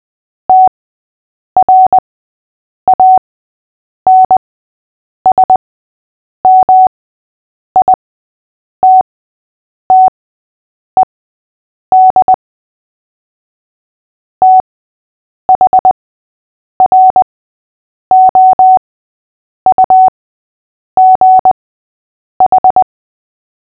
Morse code, heard as a keyed tone, about 20 wpm, spaced as for 8 wpm, 750 Hz. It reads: TRANSMITTED THROUGH
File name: cw / fists / Short-Story-6-8wpm